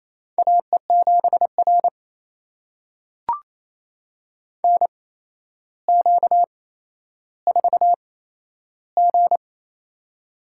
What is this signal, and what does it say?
Morse code, heard as a keyed tone, about 28 wpm, spaced as for 12 wpm, 700 Hz. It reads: AE7R NQ4G
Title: USA Call Signs - 4 Characters - 8x Character Spacing - 28wpm